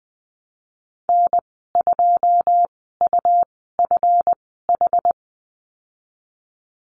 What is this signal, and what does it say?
Morse code, heard as a keyed tone, about 20 wpm, spaced as for 15 wpm, 700 Hz. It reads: N2UFH
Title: 173 Lesson 28g - Number 9 - Call Signs 20wpm